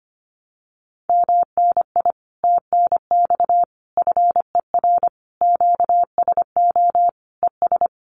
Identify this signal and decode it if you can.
Morse code, heard as a keyed tone, about 25 wpm, 700 Hz. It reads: MNI TNX FER QSO ES
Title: Common QSO Phrases Encore 25wpm